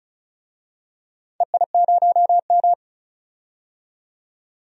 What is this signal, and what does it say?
Morse code, heard as a keyed tone, about 34 wpm, 700 Hz. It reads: EI0M
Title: DXCC-100 Call Signs - 4 Characters - 35wpm